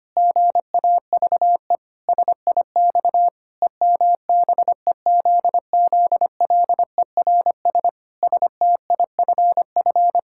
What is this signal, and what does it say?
Morse code, heard as a keyed tone, about 25 wpm, 700 Hz. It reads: GAVE SIX EMBEZZLERS STIFF
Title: Pangram Challenge v2 - 25wpm